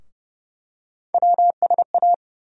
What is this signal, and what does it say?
Morse code, heard as a keyed tone, about 30 wpm, 700 Hz. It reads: WSA
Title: USA Call Sign Postfixes - 3 Letters 30wpm